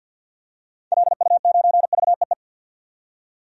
Morse code, heard as a keyed tone, about 50 wpm, 700 Hz. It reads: LU9VEE
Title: DXCC-100 Call Signs - 6 Characters - 50wpm